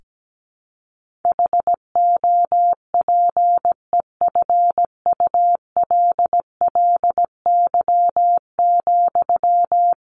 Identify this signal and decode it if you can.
Morse code, heard as a keyed tone, about 17 wpm, 700 Hz. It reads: HOPEFULLY,
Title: Sentences from Top 1500 Words 17wpm